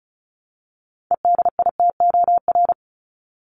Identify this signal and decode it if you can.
Morse code, heard as a keyed tone, about 34 wpm, 700 Hz. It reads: EDITOR
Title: Top 1501-1750 Words 35wpm